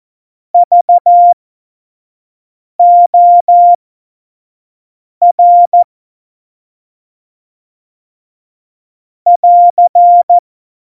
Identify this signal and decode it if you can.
Morse code, heard as a keyed tone, about 14 wpm, spaced as for 5 wpm, 700 Hz. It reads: VOR <AR>
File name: Morsekurs / Witze / Witz 15. Dezember, F70